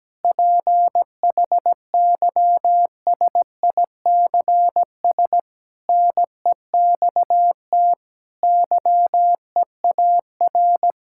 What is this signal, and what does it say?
Morse code, heard as a keyed tone, about 17 wpm, 700 Hz. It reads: PHYSICS NEXT YEAR
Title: Sentences from Top 1000 Words - Review Part 5 - 17wpm